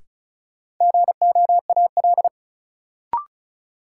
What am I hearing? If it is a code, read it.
Morse code, heard as a keyed tone, about 36 wpm, 700 Hz. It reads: GOAL
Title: Sentences from Top 1000 Words - Review Part 5 - New Word Drill - 35wpm